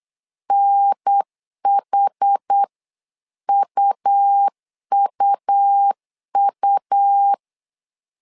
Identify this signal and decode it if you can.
Morse code, heard as a keyed tone, about 8 wpm, 800 Hz. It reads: NHUUU
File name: LEZIONI / 4 Q1S 8wpm 2710